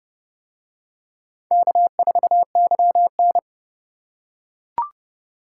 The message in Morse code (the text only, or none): K4YN